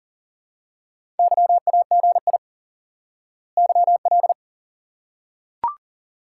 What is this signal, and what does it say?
Morse code, heard as a keyed tone, about 37 wpm, 700 Hz. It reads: YAGI YL
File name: IQE Course 3 - Lesson 89 - YAGI - Two Elements - 40wpm